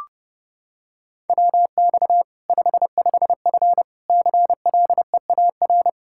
Morse code, heard as a keyed tone, about 32 wpm, 700 Hz. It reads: WX 55F CLEAR
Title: Practice QSO - Exchange 2 - Info - 30wpm